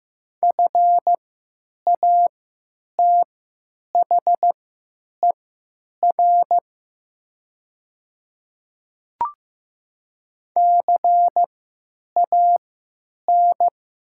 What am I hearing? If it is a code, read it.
Morse code, heard as a keyed tone, about 15 wpm, spaced as for 9 wpm, 700 Hz. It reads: FATHER CAN
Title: 3x Character Spacing - Top 300 Words - 15wpm